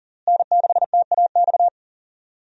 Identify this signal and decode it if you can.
Morse code, heard as a keyed tone, about 40 wpm, 700 Hz. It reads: N6TAX
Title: USA Call Signs - 5 Characters - Repeated 3x - 40wpm